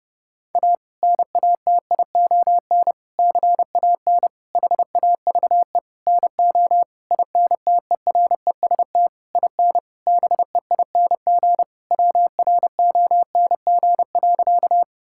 Morse code, heard as a keyed tone, about 30 wpm, 700 Hz. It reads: A NATION CAN HAVE NO INTEREST IN BEING WRONG.
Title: Sentences from Top 1000 Words - Review Part 6 - 30wpm